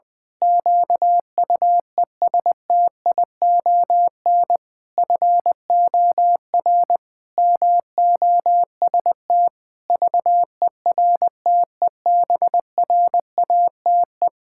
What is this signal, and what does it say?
Morse code, heard as a keyed tone, about 20 wpm, 700 Hz. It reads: QUESTION FOR MOST VERTEBRATE
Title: Pangram Challenge v2 - 20wpm